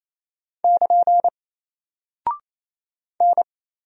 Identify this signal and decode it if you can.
Morse code, heard as a keyed tone, about 27 wpm, 700 Hz. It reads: <KN> N
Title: Single Character (Letter:Number:Punctuation:Prosign) 28wpm